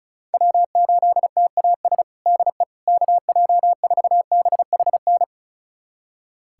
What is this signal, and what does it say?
Morse code, heard as a keyed tone, about 34 wpm, 700 Hz. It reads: W8TAS DE KJ4BHN
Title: USA Callsign DE Callsign 35wpm